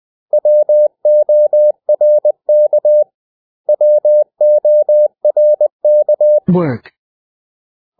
Morse code, heard as a keyed tone, about 20 wpm, 600 Hz. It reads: WORK WORK
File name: cw / 100 Common English Words